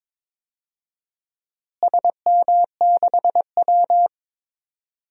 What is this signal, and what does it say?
Morse code, heard as a keyed tone, about 22 wpm, 700 Hz. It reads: SM6W
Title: DXCC-100 Call Signs - 4 Characters - Repeated 3x - 22wpm